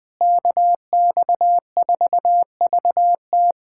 Morse code, heard as a keyed tone, about 20 wpm, 700 Hz. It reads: KX4VT
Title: USA Callsign DE Callsign 20wpm